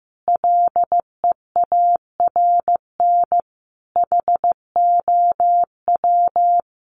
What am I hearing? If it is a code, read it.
Morse code, heard as a keyed tone, about 15 wpm, 700 Hz. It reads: LEARN HOW
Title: Sentences from Top 1200 Words 15wpm